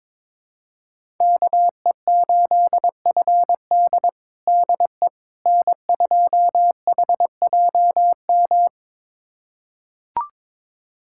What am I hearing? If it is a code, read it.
Morse code, heard as a keyed tone, about 22 wpm, 700 Hz. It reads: KE8FD DE N2HJM
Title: USA Callsign DE Callsign - 5 Characters - 22wpm